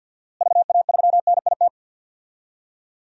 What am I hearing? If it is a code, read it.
Morse code, heard as a keyed tone, about 51 wpm, 700 Hz. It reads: VA3NIT
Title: DXCC-100 Call Signs - 6 Characters - Repeated 2x - 50wpm